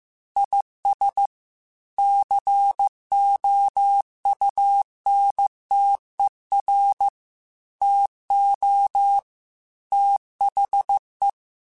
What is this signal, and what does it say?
Morse code, heard as a keyed tone, about 15 wpm, 800 Hz. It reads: IS COUNTER TO THE
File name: Technical Subjects / Technical Subjects-mp3s / XL @13